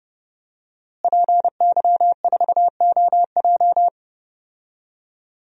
Morse code, heard as a keyed tone, about 30 wpm, 700 Hz. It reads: PY4OJ
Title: DXCC-100 Call Signs - 5 Characters - Repeated 3x - 30wpm